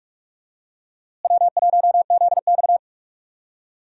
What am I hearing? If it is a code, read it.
Morse code, heard as a keyed tone, about 48 wpm, 700 Hz. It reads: W1ZX
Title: USA Call Signs - 4 Characters - Repeated 2x - 45wpm